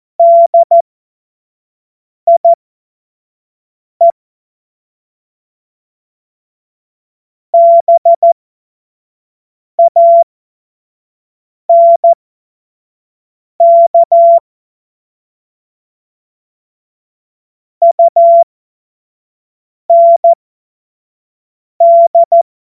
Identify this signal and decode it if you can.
Morse code, heard as a keyed tone, about 14 wpm, spaced as for 5 wpm, 680 Hz. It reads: DIE BANK UND
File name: Morsekurs / Witze / Witz 02. Mai 2023 25 F70